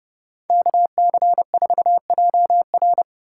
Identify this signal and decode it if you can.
Morse code, heard as a keyed tone, about 40 wpm, spaced as for 35 wpm, 700 Hz. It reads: KC4JR
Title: Speed-Racing Common QSO Phrases - 30wpm